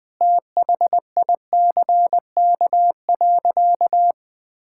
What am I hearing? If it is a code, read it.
Morse code, heard as a keyed tone, about 20 wpm, 700 Hz. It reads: THICK.
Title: Sentences from Top 1600 Words 20wpm